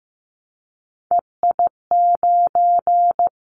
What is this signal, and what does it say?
Morse code, heard as a keyed tone, about 15 wpm, 700 Hz. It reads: EI9
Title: Callsign Prefixes for Top 50 DXCC - 15wpm